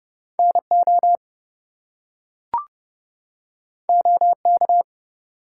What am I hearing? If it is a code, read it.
Morse code, heard as a keyed tone, about 30 wpm, 700 Hz. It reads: NO OK
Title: Two Letter Words - 30wpm